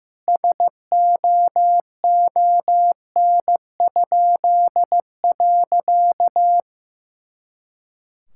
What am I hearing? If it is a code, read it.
Morse code, heard as a keyed tone, about 14 wpm, 700 Hz. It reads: SOON?.
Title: Pangram Challenge v2 - 15wpm